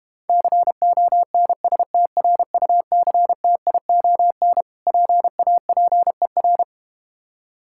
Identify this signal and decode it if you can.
Morse code, heard as a keyed tone, about 32 wpm, 700 Hz. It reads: CONSTRUCTION PAPER